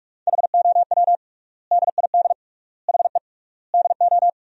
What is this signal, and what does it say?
Morse code, heard as a keyed tone, about 45 wpm, 700 Hz. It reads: HOW DID HE DO
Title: 3x Word-Spacing - Sentences from Top 100 Words - 45wpm